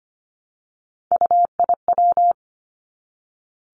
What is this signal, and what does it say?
Morse code, heard as a keyed tone, about 25 wpm, 700 Hz. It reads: UIW
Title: Three Challenging Letter-Number Combinations 25wpm